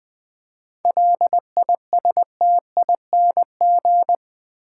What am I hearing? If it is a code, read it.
Morse code, heard as a keyed tone, about 20 wpm, 700 Hz. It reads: LISTING
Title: 7 Letter Words - First 3 Letters Given - 20wpm